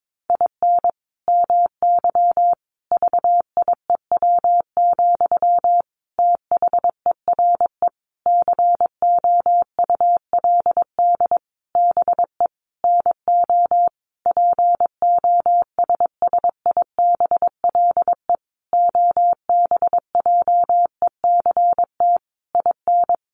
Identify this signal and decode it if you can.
Morse code, heard as a keyed tone, about 22 wpm, 700 Hz. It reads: IN MY VIEW, THERE COULD BE NO POSSIBLE OBJECT IN
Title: Sentences from Top 1500 Words 22wpm